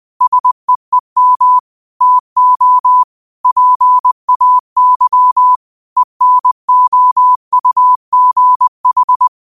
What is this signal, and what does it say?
Morse code, heard as a keyed tone, about 20 wpm, 1000 Hz. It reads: SEEM TO PAY ENOUGH
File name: qotdfiles / 20wpm